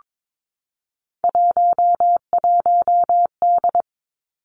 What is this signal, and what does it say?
Morse code, heard as a keyed tone, about 22 wpm, 700 Hz. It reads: 11D
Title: Three Challenging Letter-Number Combinations 22wpm